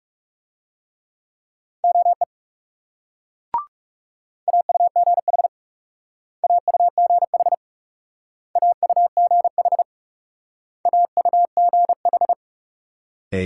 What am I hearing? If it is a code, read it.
Morse code, heard as a keyed tone, about 38 wpm, 700 Hz. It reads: OE AUGH AUGH AUGH AUGH
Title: Speed-Racing Multi-letter Phonemes - 30wpm